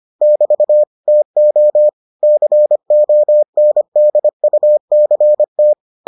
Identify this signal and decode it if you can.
Morse code, heard as a keyed tone, about 25 wpm, 600 Hz. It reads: <BT> TO CONDUCT